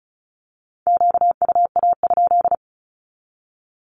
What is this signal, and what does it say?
Morse code, heard as a keyed tone, about 34 wpm, 700 Hz. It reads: QUA?
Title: Q-Codes 35wpm